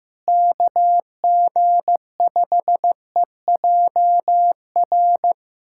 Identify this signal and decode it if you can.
Morse code, heard as a keyed tone, about 15 wpm, 700 Hz. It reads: KG5EJR
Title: USA Callsign DE Callsign - 6 Characters - 15wpm